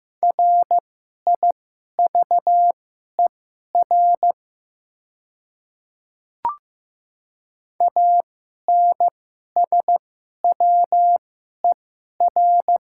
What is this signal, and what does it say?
Morse code, heard as a keyed tone, about 15 wpm, spaced as for 11 wpm, 700 Hz. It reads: RIVER ANSWER
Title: 2x Character Spacing - Top 300 Words - 15wpm